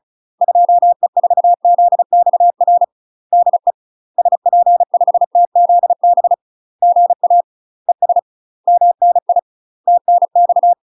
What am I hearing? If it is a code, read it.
Morse code, heard as a keyed tone, about 34 wpm, 700 Hz. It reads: 2E4ZXR DE SP5TZB GA ES MNI TNX
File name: UK MASTER QSOs TRAINING / 35WPM / UK MASTER QSOs 0024